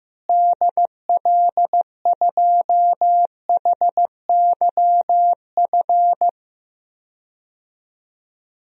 Morse code, heard as a keyed tone, about 16 wpm, 700 Hz. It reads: DL2HYF
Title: DXCC-100 Call Signs - 6 Characters - Repeated 2x - 15wpm